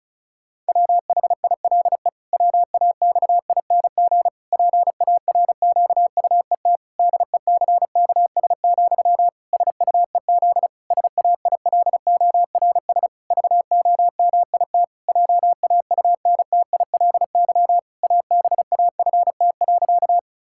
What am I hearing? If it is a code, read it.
Morse code, heard as a keyed tone, about 34 wpm, 700 Hz. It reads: WHILE WAXING PARQUET DECKS, SUEZ SAILORS VOMIT JAUNTILY ABAFT.
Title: Pangram Challenge v2 - 35wpm